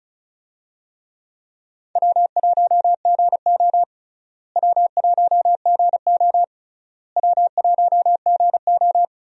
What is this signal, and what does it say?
Morse code, heard as a keyed tone, about 34 wpm, 700 Hz. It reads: W1GO W1GO W1GO
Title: USA Call Signs - 4 Characters - Repeated 3x - 35wpm